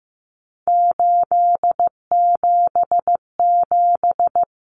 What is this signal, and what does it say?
Morse code, heard as a keyed tone, about 15 wpm, 700 Hz. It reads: 877
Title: Common Cut Numbers - 3-Digit 15wpm